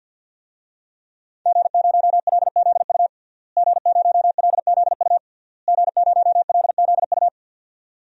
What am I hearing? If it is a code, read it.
Morse code, heard as a keyed tone, about 52 wpm, 700 Hz. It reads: G0LZU G0LZU G0LZU
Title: DXCC-100 Call Signs - 5 Characters - Repeated 3x - 50wpm